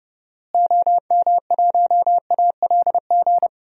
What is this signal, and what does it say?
Morse code, heard as a keyed tone, about 31 wpm, 700 Hz. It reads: OM1ALG
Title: DXCC-100 Call Signs - 6 Characters - Repeated 2x - 30wpm